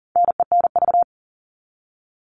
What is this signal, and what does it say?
Morse code, heard as a keyed tone, about 40 wpm, 700 Hz. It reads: NENV
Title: All Cut Numbers - 4-Digit 40wpm